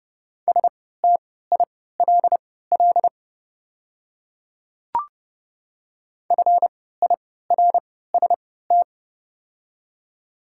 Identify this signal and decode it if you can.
Morse code, heard as a keyed tone, about 30 wpm, spaced as for 17 wpm, 700 Hz. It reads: STILL FIRST
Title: 3x Character Spacing - Top 300 Words - 30wpm